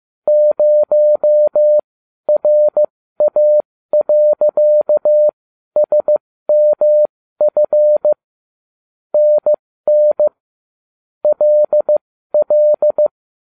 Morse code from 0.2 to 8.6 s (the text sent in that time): 0RA.SMF